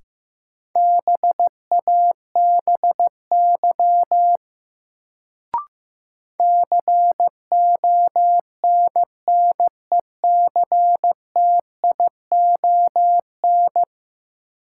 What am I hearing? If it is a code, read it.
Morse code, heard as a keyed tone, about 15 wpm, 700 Hz. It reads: BABY CONNECTION
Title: Sentences from Top 1300 Words - New Word Drill Alt - 15wpm